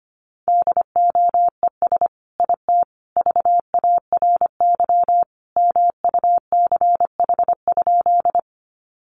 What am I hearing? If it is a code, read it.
Morse code, heard as a keyed tone, about 25 wpm, 700 Hz. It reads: DOES IT VARY MUCH?
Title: Sentences from Top 1200 Words 25wpm